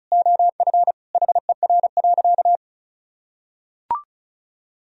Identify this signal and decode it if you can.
Morse code, heard as a keyed tone, about 36 wpm, 700 Hz. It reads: OF HER.
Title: Sentences from Top 1000 Words - Review Part 2 - 35wpm